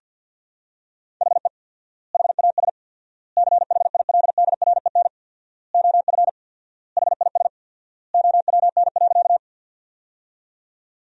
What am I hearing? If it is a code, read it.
Morse code, heard as a keyed tone, about 50 wpm, 700 Hz. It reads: HE HAS CHILDREN OF HIS OWN.
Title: 4x Word-Spacing - Sentences from Top 300 Words - 50wpm